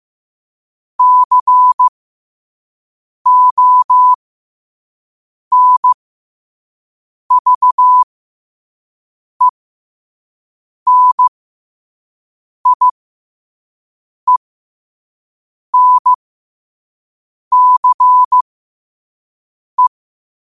Morse code, heard as a keyed tone, about 15 wpm, spaced as for 5 wpm, 1000 Hz. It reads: CONVENIENCE
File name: qotdfiles / 5wpm